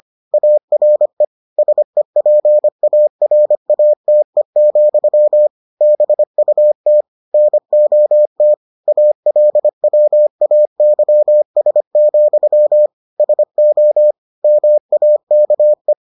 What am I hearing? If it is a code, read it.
Morse code, heard as a keyed tone, about 25 wpm, 600 Hz. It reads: ARE SEPARATE, BUT NOT ALWAYS, SO MAKE